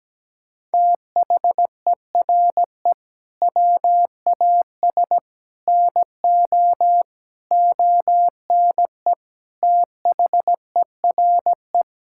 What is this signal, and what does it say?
Morse code, heard as a keyed tone, about 17 wpm, 700 Hz. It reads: THERE WAS NO ONE THERE
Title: Sentences from Top 1600 Words 17wpm